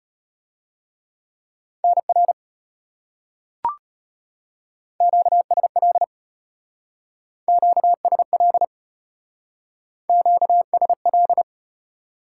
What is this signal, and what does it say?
Morse code, heard as a keyed tone, about 33 wpm, 700 Hz. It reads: NR QSL QSL QSL